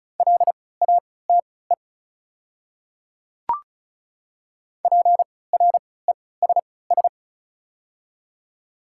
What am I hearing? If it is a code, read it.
Morse code, heard as a keyed tone, about 34 wpm, spaced as for 20 wpm, 700 Hz. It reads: LATE PRESS
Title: 3x Character Spacing - Top 300 Words - 35wpm